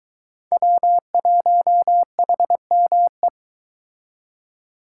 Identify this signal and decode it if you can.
Morse code, heard as a keyed tone, about 23 wpm, 700 Hz. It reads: W1HME